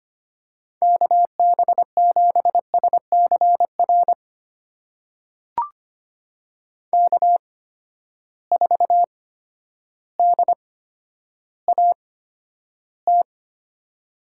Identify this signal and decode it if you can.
Morse code, heard as a keyed tone, about 25 wpm, spaced as for 11 wpm, 700 Hz. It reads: KB7SCR K4DAT
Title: USA Call Signs - 8x Character Spacing - 25wpm